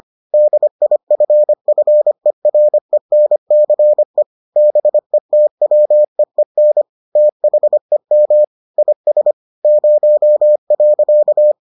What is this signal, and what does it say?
Morse code, heard as a keyed tone, about 25 wpm, 600 Hz. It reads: DIFFERENCE BETWEEN THEM IS 0.